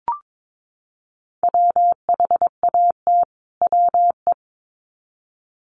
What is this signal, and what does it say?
Morse code, heard as a keyed tone, about 22 wpm, 700 Hz. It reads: WHAT WE
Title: Sets of 2 Words 22wpm